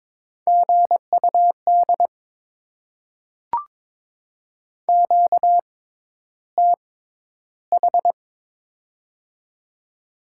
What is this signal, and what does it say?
Morse code, heard as a keyed tone, about 22 wpm, spaced as for 11 wpm, 690 Hz. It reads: GUD QTH
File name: Top 20 QSO Elements - 6x Character Spacing - 22wpm